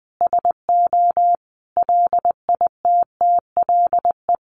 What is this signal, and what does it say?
Morse code, heard as a keyed tone, about 19 wpm, 700 Hz. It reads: SO LITTLE
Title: Sentences from Top 1200 Words 20wpm